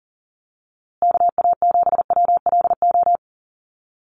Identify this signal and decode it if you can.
Morse code, heard as a keyed tone, about 40 wpm, 700 Hz. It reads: KA7WLO
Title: USA Call Signs Repeated 2x 40wpm